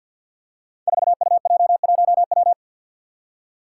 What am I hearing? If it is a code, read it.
Morse code, heard as a keyed tone, about 51 wpm, 700 Hz. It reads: 4UJ1W
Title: Five Challenging Letter-Number Combinations 50wpm